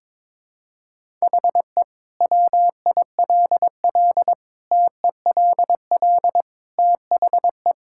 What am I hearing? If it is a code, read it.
Morse code, heard as a keyed tone, about 21 wpm, 700 Hz. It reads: HE WILL TELL THE
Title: Sentences from Top 1000 Words - Review Part 5 - 22wpm